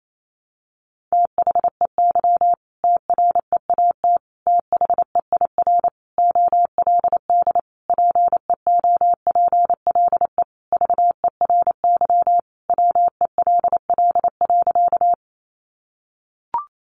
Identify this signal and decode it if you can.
Morse code, heard as a keyed tone, about 28 wpm, 700 Hz. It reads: THEY TREAT THEIR OLD PEOPLE VERY WELL.
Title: Sentences from Top 1300 Words 28wpm